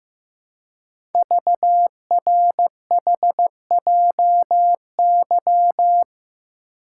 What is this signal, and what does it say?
Morse code, heard as a keyed tone, about 15 wpm, 700 Hz. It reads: VRHJY
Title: Code Groups - 5 Letter:Numbers - 15wpm